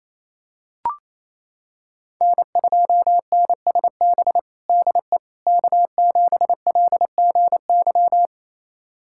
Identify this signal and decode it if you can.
Morse code, heard as a keyed tone, about 28 wpm, 700 Hz. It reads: N2NSB DE K7LGY
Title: USA Callsign DE Callsign - 5 Characters - 28wpm